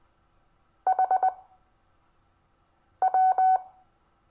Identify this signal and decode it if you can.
Morse code, heard as a keyed tone, about 21 wpm, spaced as for 5 wpm, 750 Hz. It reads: HW
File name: cw / Lesson22 REVIEW